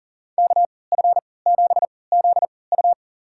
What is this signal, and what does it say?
Morse code, heard as a keyed tone, about 40 wpm, spaced as for 23 wpm, 700 Hz. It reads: KF7ZU
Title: USA Call Signs - 5 Characters - 3x Character Spacing - 40wpm